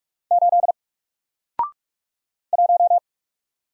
Morse code, heard as a keyed tone, about 48 wpm, 700 Hz. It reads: 8 1
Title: Single Character (Letter:Number:Punctuation:Prosign) Alt 45wpm